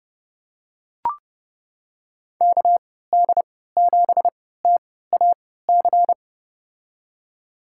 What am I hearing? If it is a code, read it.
Morse code, heard as a keyed tone, about 32 wpm, spaced as for 17 wpm, 700 Hz. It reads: KD7TAC
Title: USA Call Signs - 6 Characters - 3x Character Spacing - 30wpm